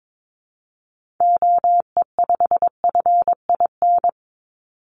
Morse code, heard as a keyed tone, about 22 wpm, 700 Hz. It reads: OE5FIN
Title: Call Signs - 22wpm